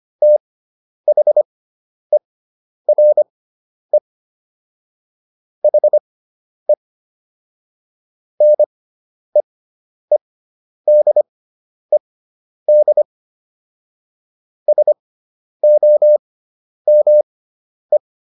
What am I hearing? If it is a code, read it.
Morse code, heard as a keyed tone, about 25 wpm, spaced as for 10 wpm, 600 Hz. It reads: THERE HE NEEDED SOME